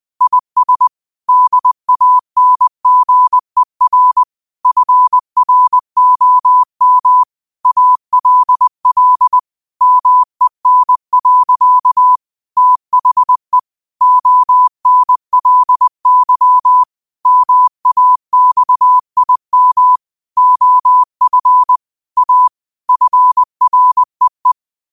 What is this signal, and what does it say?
Morse code, heard as a keyed tone, about 20 wpm, 1000 Hz. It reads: IS DANGER FROM ALL MEN. THE ONLY MAXIM OF A FREE